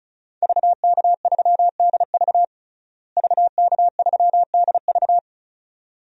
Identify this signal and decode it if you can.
Morse code, heard as a keyed tone, about 36 wpm, 700 Hz. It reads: VK3DV VK3DV
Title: DXCC-100 Call Signs - 5 Characters - Repeated 2x - 35wpm